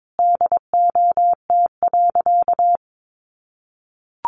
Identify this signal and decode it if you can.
Morse code, heard as a keyed tone, about 22 wpm, 700 Hz. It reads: DOT.
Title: Sentences from Top 1500 Words 22wpm